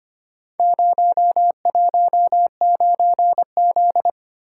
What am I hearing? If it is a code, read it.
Morse code, heard as a keyed tone, about 25 wpm, 700 Hz. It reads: 019Z
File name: practice / 25wpm / Four Letter-Number Combinations - 25wpm